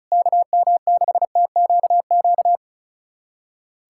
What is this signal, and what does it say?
Morse code, heard as a keyed tone, about 36 wpm, 700 Hz. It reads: KM6TQQ
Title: USA Callsign DE Callsign 35wpm